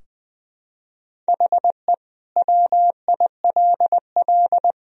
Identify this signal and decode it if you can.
Morse code, heard as a keyed tone, about 21 wpm, 700 Hz. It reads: HE WILL
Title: Sentences from Top 1000 Words - Review Part 4 - 20wpm